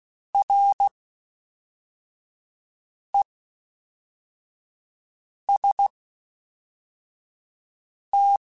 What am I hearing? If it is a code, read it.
Morse code, heard as a keyed tone, about 16 wpm, 800 Hz. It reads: R E S T
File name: Combinations / Combinations-mp3s / Combinations@10